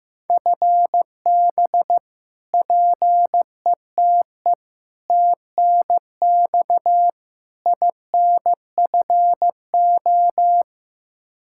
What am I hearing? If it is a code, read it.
Morse code, heard as a keyed tone, about 15 wpm, 700 Hz. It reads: FB PETE TNX INFO
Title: Practice QSO - Exchange 3 - More Info - 15wpm